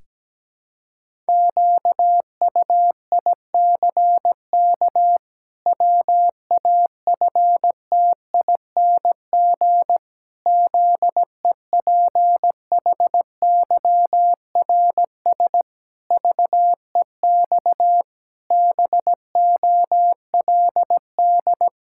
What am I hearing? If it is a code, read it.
Morse code, heard as a keyed tone, about 17 wpm, 700 Hz. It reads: QUICK WAFTING ZEPHYRS VEX BOLD
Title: Pangram Challenge v2 - 17wpm